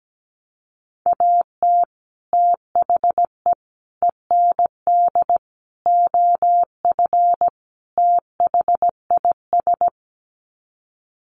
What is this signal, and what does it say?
Morse code, heard as a keyed tone, about 17 wpm, 700 Hz. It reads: AT THE END OF THIS
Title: Sets of 5 Words 17wpm